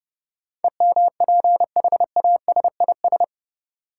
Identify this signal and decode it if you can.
Morse code, heard as a keyed tone, about 30 wpm, 700 Hz. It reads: EMPHASIS
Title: Sentences from Top 1400 Words - New Word Drill - 30wpm